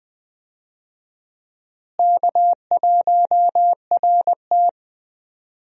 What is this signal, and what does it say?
Morse code, heard as a keyed tone, about 20 wpm, 700 Hz. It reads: K1RT
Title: USA Call Signs - 4 Characters - Repeated 3x - 20wpm